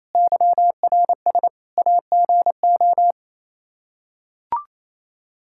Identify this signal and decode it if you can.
Morse code, heard as a keyed tone, about 27 wpm, 700 Hz. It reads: YRS AGO
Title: Practice QSO - Exchange 3 - More Info - 28wpm